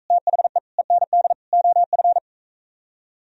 Morse code, heard as a keyed tone, about 44 wpm, 700 Hz. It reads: THE END OF